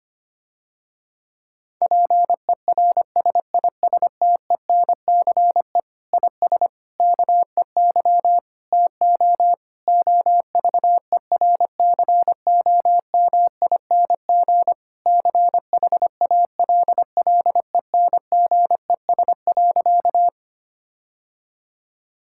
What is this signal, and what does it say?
Morse code, heard as a keyed tone, about 25 wpm, 700 Hz. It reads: PERSISTENCE IS KEY TO OVERCOMING CHALLENGES.
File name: practice / 25wpm / Difficult Sentences with Hints - 25wpm